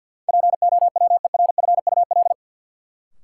Morse code, heard as a keyed tone, about 53 wpm, 700 Hz. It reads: POWERFUL
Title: Sets of 5 Words Encore - New Word Drill - 50wpm